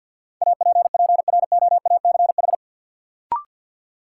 Morse code, heard as a keyed tone, about 47 wpm, 700 Hz. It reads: APPROACH
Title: Sentences from Top 1000 Words - Review Part 6 - New Word Drill - 50wpm